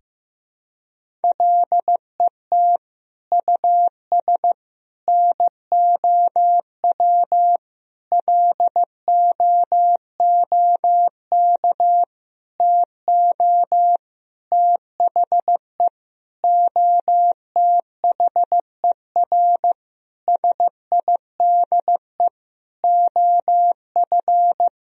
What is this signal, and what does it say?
Morse code, heard as a keyed tone, about 15 wpm, 700 Hz. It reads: LET US NOW LOOK TO THE OTHER SIDE OF
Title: Sentences from Top 1000 Words - Review Part 1 - 15wpm